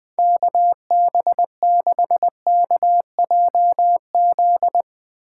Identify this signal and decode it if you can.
Morse code, heard as a keyed tone, about 21 wpm, 700 Hz. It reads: KB6KJZ